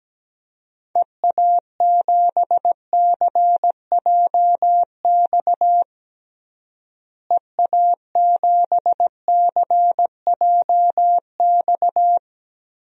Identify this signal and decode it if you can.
Morse code, heard as a keyed tone, about 17 wpm, 700 Hz. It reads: EA7CJX EA7CJX
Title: DXCC-100 Call Signs - 6 Characters - Repeated 3x - 17wpm